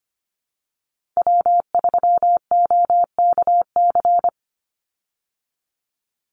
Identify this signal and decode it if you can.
Morse code, heard as a keyed tone, about 25 wpm, 700 Hz. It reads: W3OKC
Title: Call Signs Repeated 2x - 25wpm